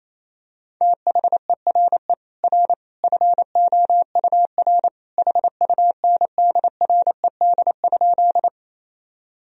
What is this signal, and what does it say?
Morse code, heard as a keyed tone, about 28 wpm, 700 Hz. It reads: THERE R FOUR HUNDRED?
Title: Sentences from Top 400 Words and Abbreviations 28wpm